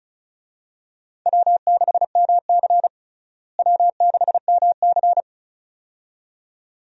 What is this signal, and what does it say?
Morse code, heard as a keyed tone, about 36 wpm, 700 Hz. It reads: W6MC W6MC
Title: USA Call Signs - 4 Characters - Repeated 2x - 35wpm